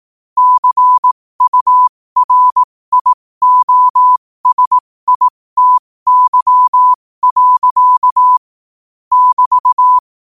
Morse code, heard as a keyed tone, about 18 wpm, spaced as for 15 wpm, 1000 Hz. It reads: CURIOSITY. <BT>
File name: qotdfiles / 15wpm